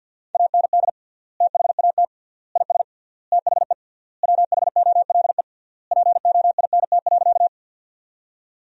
Morse code, heard as a keyed tone, about 48 wpm, 700 Hz. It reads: AND THAT IS THE WHOLE POINT.
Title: 3x Word-Spacing - Sentences from Top 400 Words - 50wpm